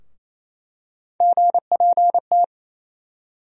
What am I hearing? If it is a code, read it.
Morse code, heard as a keyed tone, about 28 wpm, 700 Hz. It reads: GPT
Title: USA Call Sign Postfixes - 3 Letters 28wpm